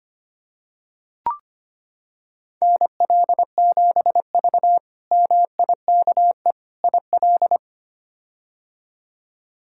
Morse code, heard as a keyed tone, about 25 wpm, 700 Hz. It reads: NL7V MIKE IL
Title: K1USN SST QSO Practice - 25wpm